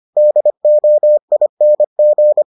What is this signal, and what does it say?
Morse code, heard as a keyed tone, about 25 wpm, 600 Hz. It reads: DOING